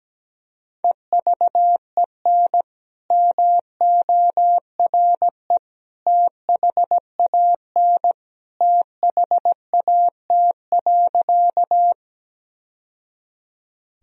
Morse code, heard as a keyed tone, about 17 wpm, 700 Hz. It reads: EVEN MORE THAN THAT.
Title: Sentences from Top 1000 Words - Review Part 1 - 17wpm